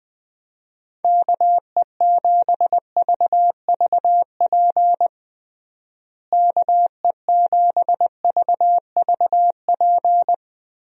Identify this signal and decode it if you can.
Morse code, heard as a keyed tone, about 20 wpm, 700 Hz. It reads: KE7VVP KE7VVP
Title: USA Call Signs - 6 Characters - Repeated 3x - 20wpm